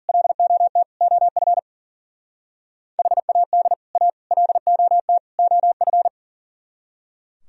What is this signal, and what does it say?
Morse code, heard as a keyed tone, about 51 wpm, 700 Hz. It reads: LOT OF HAD A LOT OF